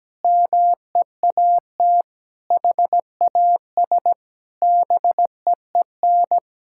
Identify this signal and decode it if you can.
Morse code, heard as a keyed tone, about 17 wpm, 700 Hz. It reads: MEAT HAS BEEN
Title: Sentences from Top 1000 Words - Review Part 5 - 17wpm